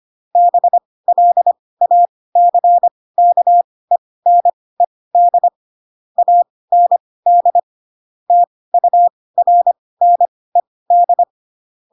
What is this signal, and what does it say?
Morse code, heard as a keyed tone, about 25 wpm, spaced as for 18 wpm, 700 Hz. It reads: BLACKENED AND TURNED
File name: Pinocchio / Pinocchio ch6 12 25 ik1hge